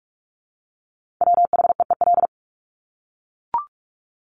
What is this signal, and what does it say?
Morse code, heard as a keyed tone, about 44 wpm, 700 Hz. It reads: WHEEL
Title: Top 500 Words Repeated 45wpm